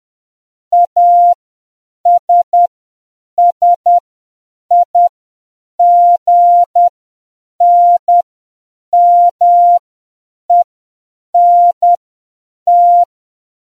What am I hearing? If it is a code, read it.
Morse code, heard as a keyed tone, about 10 wpm, spaced as for 7 wpm, 700 Hz. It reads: ASSIGNMENT